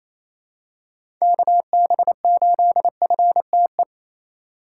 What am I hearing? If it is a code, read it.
Morse code, heard as a keyed tone, about 28 wpm, 700 Hz. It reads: KB8FTE
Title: USA Call Signs - 6 Characters - 4x Character Spacing - 28wpm